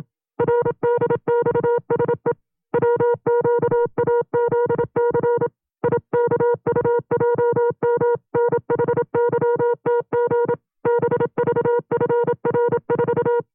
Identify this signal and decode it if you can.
Morse code, heard as a keyed tone, about 27 wpm, 480 Hz. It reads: RDXSE WQAZC IKUJM NHYTG BVFR4
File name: CW-practice/25WPM